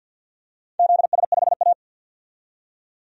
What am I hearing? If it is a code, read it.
Morse code, heard as a keyed tone, about 52 wpm, 700 Hz. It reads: ZS5A
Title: DXCC-100 Call Signs - 4 Characters - 50wpm